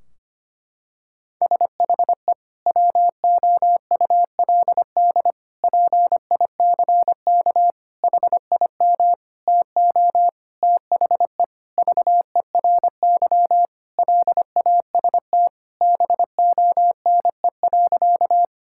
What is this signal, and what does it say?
Morse code, heard as a keyed tone, about 25 wpm, 700 Hz. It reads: SHE WOULD PICK HIM TO THE VERY LAST BONE.
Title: Sentences from Top 1600 Words 25wpm